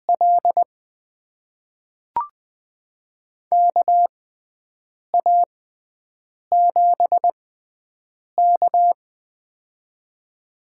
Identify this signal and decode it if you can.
Morse code, heard as a keyed tone, about 20 wpm, spaced as for 7 wpm, 700 Hz. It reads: L KA7K
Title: USA Call Signs - 6x Character Spacing - 20wpm